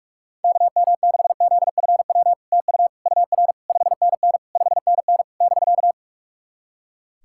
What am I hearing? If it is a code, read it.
Morse code, heard as a keyed tone, about 46 wpm, 700 Hz. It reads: KM6ZFW TU UR 5NN 5NN <BK>
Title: SOTA QSO Practice 45wpm